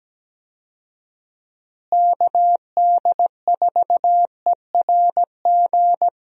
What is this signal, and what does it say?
Morse code, heard as a keyed tone, about 17 wpm, 700 Hz. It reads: KD4ERG
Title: USA Call Signs - 6 Characters - Repeated 2x - 17wpm